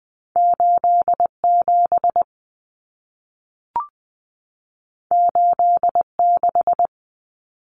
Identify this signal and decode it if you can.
Morse code, heard as a keyed tone, about 20 wpm, 700 Hz. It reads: 87 86
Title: Common Cut Numbers - 2-Digit 20wpm